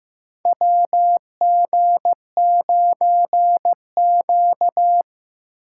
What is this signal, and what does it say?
Morse code, heard as a keyed tone, about 15 wpm, 700 Hz. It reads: WG9Q
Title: USA Call Signs - 4 Characters - 4x Character Spacing - 15wpm